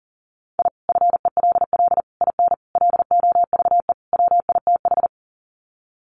Morse code, heard as a keyed tone, about 40 wpm, 700 Hz. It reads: I FELL IN LOVE WITH
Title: Sets of 5 Words Encore 40wpm